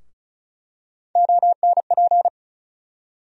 Morse code, heard as a keyed tone, about 36 wpm, 700 Hz. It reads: ONP
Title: USA Call Sign Postfixes - 3 Letters 35wpm